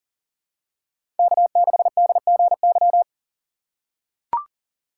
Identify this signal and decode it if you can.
Morse code, heard as a keyed tone, about 40 wpm, 700 Hz. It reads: K6DGY
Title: USA Call Signs - 5 Characters - 6x Character Spacing - 40wpm